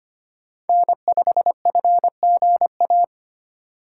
Morse code, heard as a keyed tone, about 25 wpm, 700 Hz. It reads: N5FGA
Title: USA Call Signs - 2x Character Spacing - 25wpm